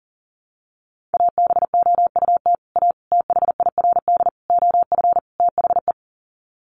Morse code, heard as a keyed tone, about 40 wpm, 700 Hz. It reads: ABOUT A THIRD OF THE
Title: Sets of 5 Words Encore2 40wpm